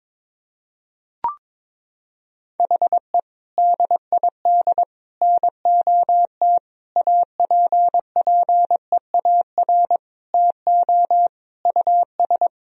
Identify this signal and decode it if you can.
Morse code, heard as a keyed tone, about 22 wpm, 700 Hz. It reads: HE DID NOT APPEAR TO US
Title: Sentences from Top 1000 Words - Review Part 2 - 22wpm